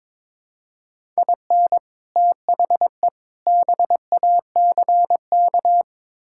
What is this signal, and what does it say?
Morse code, heard as a keyed tone, about 22 wpm, 700 Hz. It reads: IN THE BACK
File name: practice / 22wpm / Build A Phrase - 5-Word Phrases - 22wpm